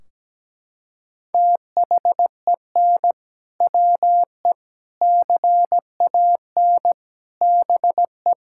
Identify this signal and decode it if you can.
Morse code, heard as a keyed tone, about 17 wpm, 700 Hz. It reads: THEN WE CAN BE
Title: Sentences from Top 1600 Words 17wpm